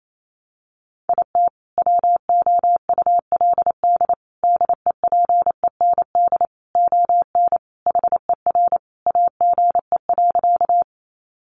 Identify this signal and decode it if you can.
Morse code, heard as a keyed tone, about 28 wpm, 700 Hz. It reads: IT WOULD DEPEND ON HER AGE.
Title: Sentences from Top 1300 Words 28wpm